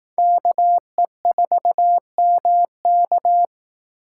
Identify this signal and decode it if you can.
Morse code, heard as a keyed tone, about 18 wpm, 700 Hz. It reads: KE4MK
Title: Speed Racing - USA 2x2 Call Signs 15wpm